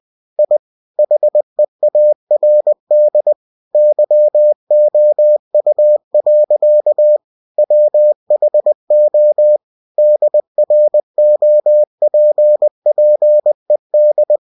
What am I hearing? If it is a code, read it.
Morse code, heard as a keyed tone, about 20 wpm, 600 Hz. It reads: I HEARD YOU. WHO DROPPED
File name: cw-books / winnie-the-pooh / winnie-20wpm-page-0105